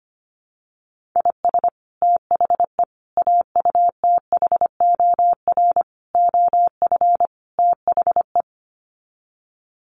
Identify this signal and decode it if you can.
Morse code, heard as a keyed tone, about 25 wpm, 700 Hz. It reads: IS THE AUTHOR OF THE
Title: Sets of 5 Words Encore 25wpm